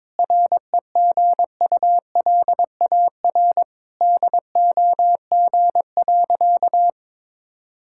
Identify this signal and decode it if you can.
Morse code, heard as a keyed tone, about 22 wpm, 700 Hz. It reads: REGULAR DOG.
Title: Sentences from Top 1000 Words - Review Part 6 - 22wpm